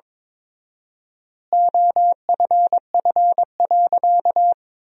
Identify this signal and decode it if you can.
Morse code, heard as a keyed tone, about 21 wpm, 700 Hz. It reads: OFF.
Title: 4x Word-Spacing - Sentences from Top 500 Words - 22wpm